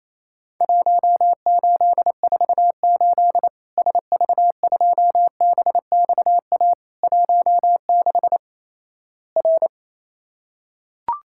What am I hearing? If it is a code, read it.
Morse code, heard as a keyed tone, about 28 wpm, 700 Hz. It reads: 1848 SV2BXA 16 R
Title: Worked All Europe DX Contest - QTC Exchange - 28wpm